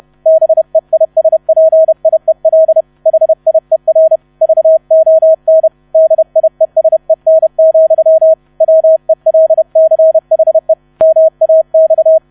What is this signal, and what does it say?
Morse code, heard as a keyed tone, about 31 wpm, 640 Hz. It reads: BEISPIEL HIER VON DIESEN, WELCHE MAX
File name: 30wpm